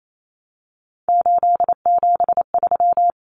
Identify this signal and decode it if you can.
Morse code, heard as a keyed tone, about 28 wpm, 700 Hz. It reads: 873
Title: Common Cut Numbers - 3-Digit 28wpm